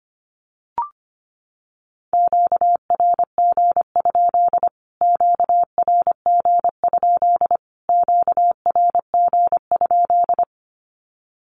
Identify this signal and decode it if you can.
Morse code, heard as a keyed tone, about 25 wpm, 700 Hz. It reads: QRG? QRG? QRG?
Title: Q-Codes Repeated 3x 25wpm